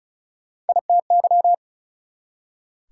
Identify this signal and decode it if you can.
Morse code, heard as a keyed tone, about 48 wpm, spaced as for 42 wpm, 700 Hz. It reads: ITY
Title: Speed-Racing Prefixes and Suffixes 35wpm